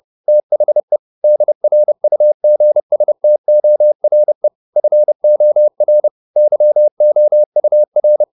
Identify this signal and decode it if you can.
Morse code, heard as a keyed tone, about 30 wpm, 600 Hz. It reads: THE DRUGSTORE FOR YOUR